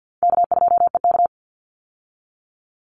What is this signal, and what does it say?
Morse code, heard as a keyed tone, about 50 wpm, 700 Hz. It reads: K2EK
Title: Easy Call Signs - 50wpm